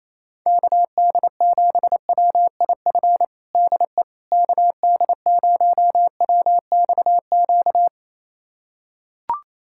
Morse code, heard as a keyed tone, about 28 wpm, 700 Hz. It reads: KD7WIF DE KD0WXQ
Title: USA Callsign DE Callsign - 6 Characters - 28wpm